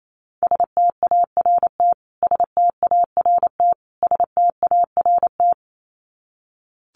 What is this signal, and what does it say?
Morse code, heard as a keyed tone, about 28 wpm, 700 Hz. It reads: START START START
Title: Top 300 Words Repeated 28wpm